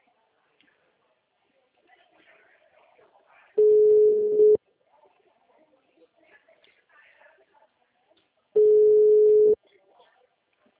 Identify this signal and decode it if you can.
Ringback tone (Europe, 425 Hz), with speech.